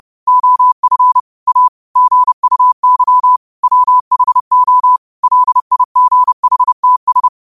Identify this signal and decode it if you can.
Morse code, heard as a keyed tone, about 30 wpm, 1000 Hz. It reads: OF A GUY WHO LIGHTS